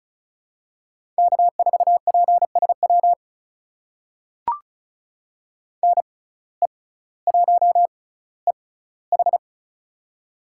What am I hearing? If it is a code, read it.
Morse code, heard as a keyed tone, about 36 wpm, spaced as for 18 wpm, 700 Hz. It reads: K4PSW NE1EH
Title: USA Call Signs - 5 Characters - 6x Character Spacing - 35wpm